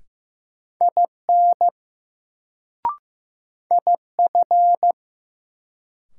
Morse code, heard as a keyed tone, about 15 wpm, 700 Hz. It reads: IN IF
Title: Sentences from Top 100 Words - New Word Drill - 15wpm